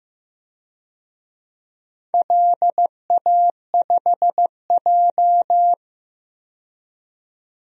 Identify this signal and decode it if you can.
Morse code, heard as a keyed tone, about 15 wpm, 700 Hz. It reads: LA5J
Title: DXCC-100 Call Signs - 4 Characters - 15wpm